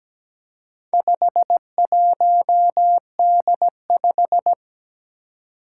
Morse code, heard as a keyed tone, about 17 wpm, 700 Hz. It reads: 51D5
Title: Four Challenging Letter-Number Combinations 17wpm